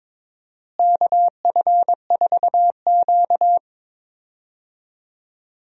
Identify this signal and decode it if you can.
Morse code, heard as a keyed tone, about 22 wpm, 700 Hz. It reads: KF4Q
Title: USA Call Signs - 4 Characters - Repeated 2x - 22wpm